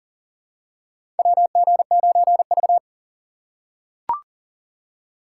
Morse code, heard as a keyed tone, about 40 wpm, 700 Hz. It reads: WG9V
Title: USA Call Signs - 4 Characters - 6x Character Spacing - 40wpm